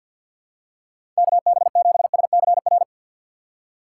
Morse code, heard as a keyed tone, about 48 wpm, 700 Hz. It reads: KB7SCR
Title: USA Call Signs - 6 Characters - 6x Character Spacing - 50wpm